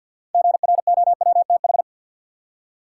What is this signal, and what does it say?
Morse code, heard as a keyed tone, about 53 wpm, 700 Hz. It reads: GROWTH
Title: Six Letter Words - 50wpm